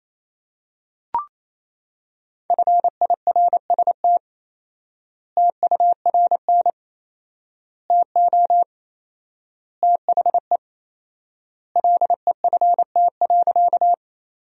Morse code, heard as a keyed tone, about 28 wpm, 700 Hz. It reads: FIRST TURN TO THE LEFT.
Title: 4x Word-Spacing - Sentences from Top 300 Words - 28wpm